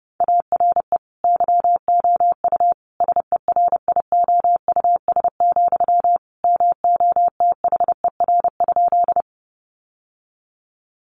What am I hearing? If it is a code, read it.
Morse code, heard as a keyed tone, about 28 wpm, 700 Hz. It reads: ARE YOU SERIOUS, MOTHER?
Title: Sentences from Top 1100 Words 30wpm